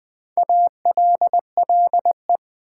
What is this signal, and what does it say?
Morse code, heard as a keyed tone, about 20 wpm, 700 Hz. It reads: ALLE
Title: Build A Word - 4 Letters At A Time - 10-Letter Words - 20wpm